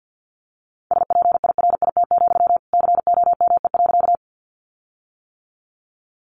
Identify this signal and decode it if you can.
Morse code, heard as a keyed tone, about 50 wpm, 700 Hz. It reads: SPIRIT, COME.
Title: Sentences from Top 1400 Words 50wpm